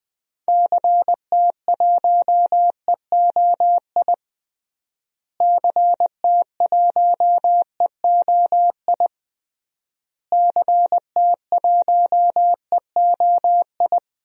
Morse code, heard as a keyed tone, about 20 wpm, 700 Hz. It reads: CT1EOI CT1EOI CT1EOI
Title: DXCC-100 Call Signs - 6 Characters - Repeated 3x - 20wpm